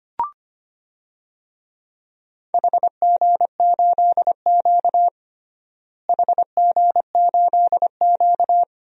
Morse code, heard as a keyed tone, about 25 wpm, 700 Hz. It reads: HG8Q HG8Q
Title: DXCC-100 Call Signs - 4 Characters - Repeated 2x - 25wpm